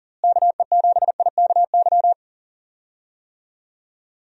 Morse code, heard as a keyed tone, about 40 wpm, 700 Hz. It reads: KE7IKY
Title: USA Call Signs - 6 Characters - Repeated 2x - 40wpm